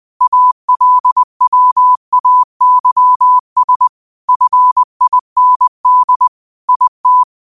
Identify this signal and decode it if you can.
Morse code, heard as a keyed tone, about 20 wpm, 1000 Hz. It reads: ALWAYS FIND IT
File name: qotdfiles / 20wpm